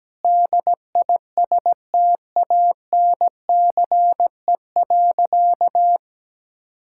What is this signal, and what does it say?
Morse code, heard as a keyed tone, about 17 wpm, 700 Hz. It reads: DISTANCE.
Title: Sentences from Top 1600 Words 17wpm